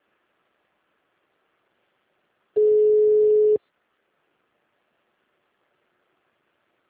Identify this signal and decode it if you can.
Ringback tone (Europe, 425 Hz).